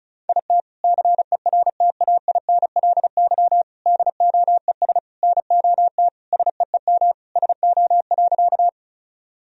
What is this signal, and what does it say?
Morse code, heard as a keyed tone, about 34 wpm, 700 Hz. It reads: IT CERTAINLY DOES NOT SEEM SO.
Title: Sentences from Top 1000 Words - Review Part 1 - 35wpm